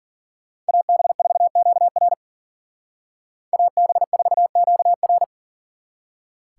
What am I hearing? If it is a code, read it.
Morse code, heard as a keyed tone, about 49 wpm, 700 Hz. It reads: AB4QR AB4QR
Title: Speed Racing - USA Call Signs 40wpm